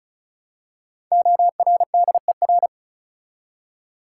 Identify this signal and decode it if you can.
Morse code, heard as a keyed tone, about 34 wpm, 700 Hz. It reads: ORDER
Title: Five Letter Words - 35wpm